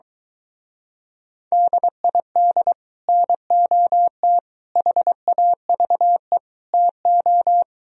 Morse code, heard as a keyed tone, about 23 wpm, 700 Hz. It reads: DID NOT HAVE TO